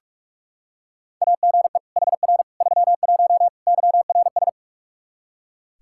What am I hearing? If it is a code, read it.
Morse code, heard as a keyed tone, about 46 wpm, 700 Hz. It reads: AGE HR 31 YRS
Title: Practice QSO - Exchange 3 - More Info - 45wpm